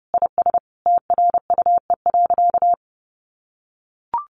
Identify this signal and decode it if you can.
Morse code, heard as a keyed tone, about 30 wpm, 700 Hz. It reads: IS TRUE.
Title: Sentences from Top 1400 Words 30wpm